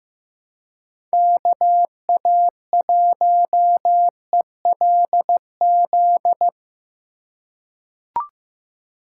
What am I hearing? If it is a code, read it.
Morse code, heard as a keyed tone, about 15 wpm, 700 Hz. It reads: KA1ELZ
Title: USA Call Signs - 6 Characters - 2x Character Spacing - 15wpm